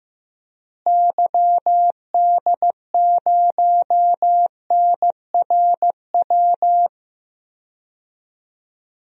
Morse code, heard as a keyed tone, about 15 wpm, 700 Hz. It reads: YD0NRW
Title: DXCC-100 Call Signs - 6 Characters - Repeated 3x - 15wpm